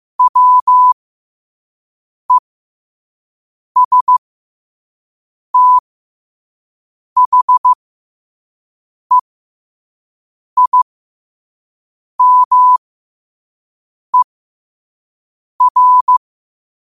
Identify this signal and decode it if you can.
Morse code, heard as a keyed tone, about 15 wpm, spaced as for 5 wpm, 1000 Hz. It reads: WESTHEIMER